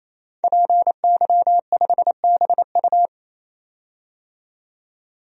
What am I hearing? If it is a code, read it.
Morse code, heard as a keyed tone, about 27 wpm, 700 Hz. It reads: PY5BU
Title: DXCC-100 Call Signs - 5 Characters - 28wpm